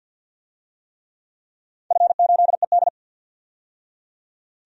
Morse code, heard as a keyed tone, about 48 wpm, 700 Hz. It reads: F8ED
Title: DXCC-100 Call Signs - 4 Characters - 50wpm